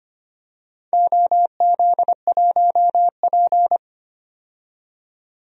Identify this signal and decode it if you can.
Morse code, heard as a keyed tone, about 25 wpm, 700 Hz. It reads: OZ1P
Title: DXCC-100 Call Signs - 4 Characters - 25wpm